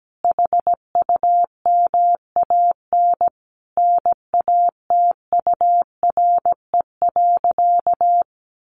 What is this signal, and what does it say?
Morse code, heard as a keyed tone, about 17 wpm, 700 Hz. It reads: HUMAN NATURE.
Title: Sentences from Top 700 Words 17wpm